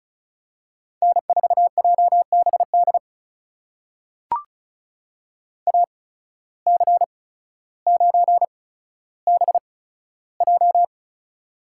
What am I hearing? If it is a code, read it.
Morse code, heard as a keyed tone, about 34 wpm, spaced as for 15 wpm, 700 Hz. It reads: N4JBD AC9BJ
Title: USA Call Signs - 5 Characters - 8x Character Spacing - 35wpm